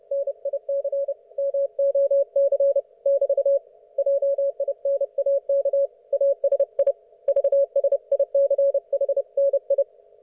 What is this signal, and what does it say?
Morse code, heard as a keyed tone, about 30 wpm, 560 Hz. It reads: NIC MOC <BT> JINAK ASI VSICHNI
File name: Archiv / CUC179 / ok1fhi cul179